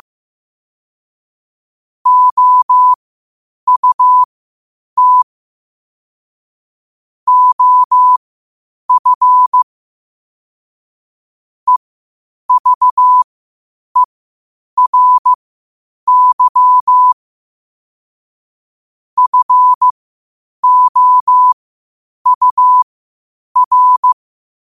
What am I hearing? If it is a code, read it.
Morse code, heard as a keyed tone, about 15 wpm, spaced as for 8 wpm, 1000 Hz. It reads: OUT OF EVERY FOUR
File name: qotdfiles / 8wpm